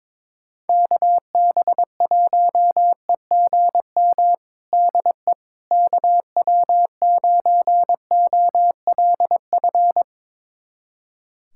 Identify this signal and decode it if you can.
Morse code, heard as a keyed tone, about 22 wpm, 700 Hz. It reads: KB1EGM DE KW9OLF
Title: USA Callsign DE Callsign 22wpm